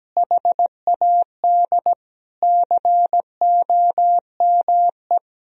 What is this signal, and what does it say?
Morse code, heard as a keyed tone, about 17 wpm, 700 Hz. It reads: HAD COME
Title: Sentences from Top 1000 Words - Review Part 1 - 17wpm